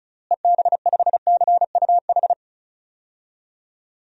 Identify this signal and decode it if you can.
Morse code, heard as a keyed tone, about 36 wpm, 700 Hz. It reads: EB5CUH
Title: DXCC-100 Call Signs - 6 Characters - Repeated 3x - 35wpm